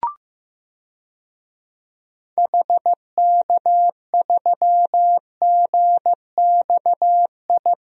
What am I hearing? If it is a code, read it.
Morse code, heard as a keyed tone, about 15 wpm, 700 Hz. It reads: HK3GXI
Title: DXCC-100 Call Signs - 6 Characters - 15wpm